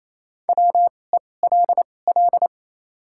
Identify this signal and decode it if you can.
Morse code, heard as a keyed tone, about 28 wpm, spaced as for 20 wpm, 700 Hz. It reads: WELL
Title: 2x Character Spacing - Top 500 Words - 28wpm